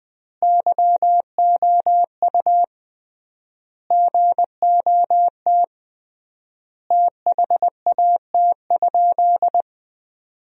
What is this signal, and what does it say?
Morse code, heard as a keyed tone, about 20 wpm, 700 Hz. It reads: YOU GOT THAT?
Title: 3x Word-Spacing - Sentences from Top 300 Words - 20wpm